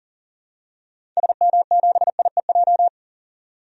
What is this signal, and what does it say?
Morse code, heard as a keyed tone, about 40 wpm, 700 Hz. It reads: SM7IEJ
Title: DXCC-100 Call Signs - 6 Characters - 40wpm